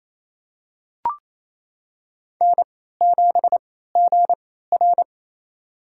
Morse code, heard as a keyed tone, about 28 wpm, spaced as for 16 wpm, 700 Hz. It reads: N7GR
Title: USA Call Signs - 4 Characters - 3x Character Spacing - 28wpm